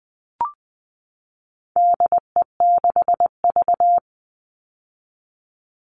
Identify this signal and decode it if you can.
Morse code, heard as a keyed tone, about 20 wpm, 700 Hz. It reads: DE6V
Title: All Cut Numbers - 4-Digit 20wpm